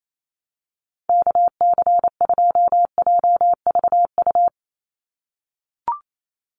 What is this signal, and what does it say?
Morse code, heard as a keyed tone, about 28 wpm, 700 Hz. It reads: KC2JVU
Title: USA Call Signs 28wpm